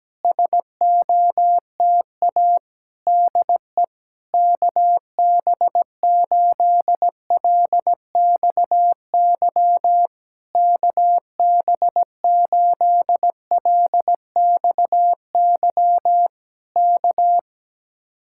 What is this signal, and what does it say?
Morse code, heard as a keyed tone, about 17 wpm, 700 Hz. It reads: SOTA DE KB8LXY KB8LXY K
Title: Speed-Racing Common QSO Phrases - 17wpm